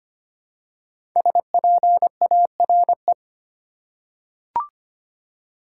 Morse code, heard as a keyed tone, about 25 wpm, 700 Hz. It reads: SPARE
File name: 5 Letter Words - First 3 Letters Given - 25wpm